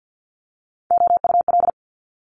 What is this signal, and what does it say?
Morse code, heard as a keyed tone, about 50 wpm, 700 Hz. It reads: OUL
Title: Multi-letter Phonemes 50wpm